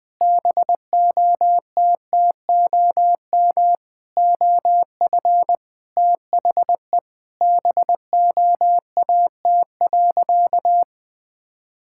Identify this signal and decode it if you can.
Morse code, heard as a keyed tone, about 20 wpm, 700 Hz. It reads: BOTTOM OF THE BOAT.
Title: Sentences from Top 1000 Words - Review Part 6 - 20wpm